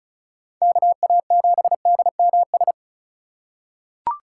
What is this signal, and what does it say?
Morse code, heard as a keyed tone, about 36 wpm, 700 Hz. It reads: KA7DMS
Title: USA Call Signs - 6 Characters - 4x Character Spacing - 35wpm